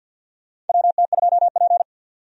Morse code, heard as a keyed tone, about 51 wpm, 700 Hz. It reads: WT2P
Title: USA Call Signs - 4 Characters - 50wpm